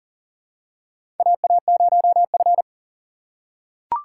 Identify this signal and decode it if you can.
Morse code, heard as a keyed tone, about 40 wpm, 700 Hz. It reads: AA0F
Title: USA Call Signs - 4 Characters - 3x Character Spacing - 40wpm